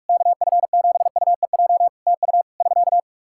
Morse code, heard as a keyed tone, about 46 wpm, 700 Hz. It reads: KF7UEJ TU <SK>
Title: Practice QSO - Exchange 4 - Ending - 45wpm